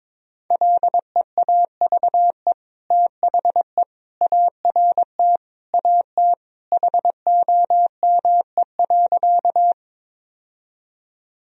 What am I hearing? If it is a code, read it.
Morse code, heard as a keyed tone, about 22 wpm, 700 Hz. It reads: LEAVE THE ART AT HOME.
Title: Sentences from Top 1000 Words - Review Part 3 - 22wpm